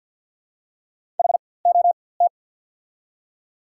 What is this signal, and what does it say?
Morse code, heard as a keyed tone, about 54 wpm, spaced as for 24 wpm, 690 Hz. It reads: HOT
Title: 4x Character Spacing - Top 300 Words - 50wpm